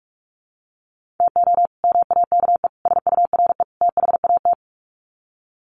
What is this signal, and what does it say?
Morse code, heard as a keyed tone, about 44 wpm, 700 Hz. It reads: TO MAKE SURE THAT
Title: Sets of 4 Words 45wpm